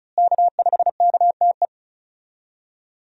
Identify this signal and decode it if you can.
Morse code, heard as a keyed tone, about 36 wpm, 700 Hz. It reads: K5KTE
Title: USA Call Signs - 5 Characters - Repeated 3x - 35wpm